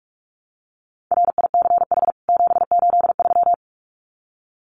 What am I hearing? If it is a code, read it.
Morse code, heard as a keyed tone, about 44 wpm, 700 Hz. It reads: RICH 783
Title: CWT Exchange 45wpm